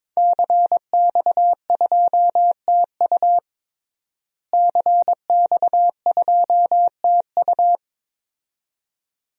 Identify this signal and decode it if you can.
Morse code, heard as a keyed tone, about 22 wpm, 700 Hz. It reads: CX2TU CX2TU
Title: DXCC-100 Call Signs - 5 Characters - Repeated 2x - 22wpm